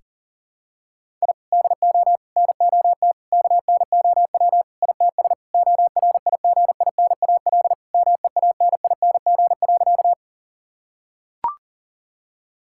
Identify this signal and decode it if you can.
Morse code, heard as a keyed tone, about 40 wpm, 700 Hz. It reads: I DO NOT KNOW ITS ORIGINAL MEANING.
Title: Sentences from Top 1000 Words - Review Part 6 - 40wpm